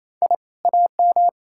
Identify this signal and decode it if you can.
Morse code, heard as a keyed tone, about 38 wpm, spaced as for 33 wpm, 700 Hz. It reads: I AM
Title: Speed-Racing Common QSO Phrases Encore - 28wpm